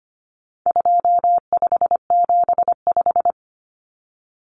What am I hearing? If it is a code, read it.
Morse code, heard as a keyed tone, about 25 wpm, 700 Hz. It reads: 2575
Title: Numbers - 4-Digit 25wpm